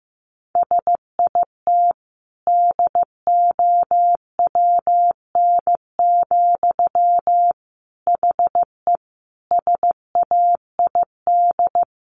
Morse code, heard as a keyed tone, about 15 wpm, 700 Hz. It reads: SIT DOWN, HE SAID
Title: Sentences from Top 1500 Words 15wpm